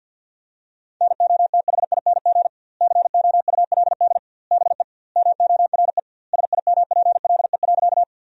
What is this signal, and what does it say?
Morse code, heard as a keyed tone, about 50 wpm, 700 Hz. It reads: NOTHING COULD BE MORE SIMPLE.
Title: 2x Word-Spacing - Sentences from Top 500 Words - 50wpm